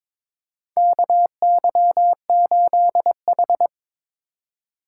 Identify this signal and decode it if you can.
Morse code, heard as a keyed tone, about 21 wpm, 700 Hz. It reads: KY8H
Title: USA Call Signs - 4x Character Spacing - 22wpm